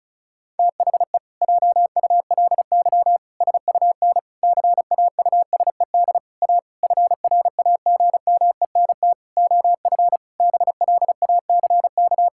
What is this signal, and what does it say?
Morse code, heard as a keyed tone, about 34 wpm, 700 Hz. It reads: THE JULY SUN CAUSED A FRAGMENT OF BLACK
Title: Pangram Challenge v2 - 35wpm